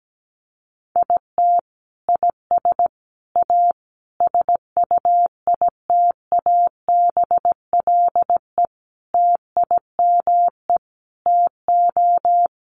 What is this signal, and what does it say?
Morse code, heard as a keyed tone, about 17 wpm, 700 Hz. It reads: IT IS A SUITABLE TIME TO
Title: Sentences from Top 1400 Words 17wpm